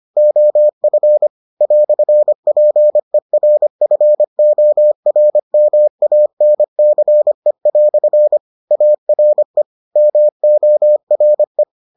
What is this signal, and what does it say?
Morse code, heard as a keyed tone, about 25 wpm, 600 Hz. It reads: OF "PERFORMANCE" ARE MORE